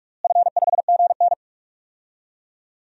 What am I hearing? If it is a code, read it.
Morse code, heard as a keyed tone, about 47 wpm, 700 Hz. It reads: F5GN
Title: DXCC-100 Call Signs - 4 Characters - Repeated 3x - 45wpm